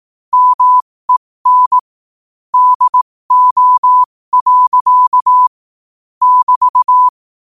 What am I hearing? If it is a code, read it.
Morse code, heard as a keyed tone, about 18 wpm, spaced as for 15 wpm, 1000 Hz. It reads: MEN DO. <BT>